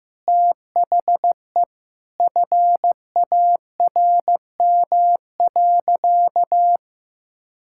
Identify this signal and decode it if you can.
Morse code, heard as a keyed tone, about 16 wpm, 700 Hz. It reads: THE FARM.
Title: Sentences from Top 1000 Words - Review Part 6 - 15wpm